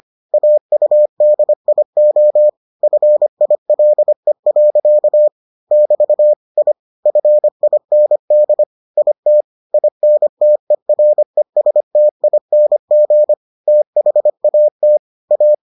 Morse code, heard as a keyed tone, about 25 wpm, 600 Hz. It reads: AUDIO FILE. <BT> I FIND IT INTERESTING THAT A